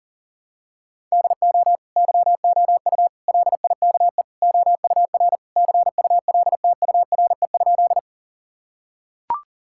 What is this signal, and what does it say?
Morse code, heard as a keyed tone, about 37 wpm, 690 Hz. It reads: DO YOU LIKE OUR CULTURE?
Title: Sentences from Top 1000 Words - Review Part 6 - 40wpm